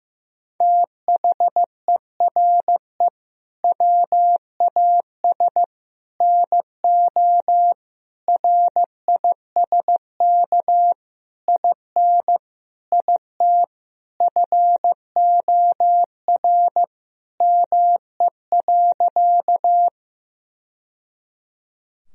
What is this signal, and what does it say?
Morse code, heard as a keyed tone, about 15 wpm, 700 Hz. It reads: THERE WAS NO RISK IN IT FOR ME.
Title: Sentences from Top 1000 Words - Review Part 4 - 15wpm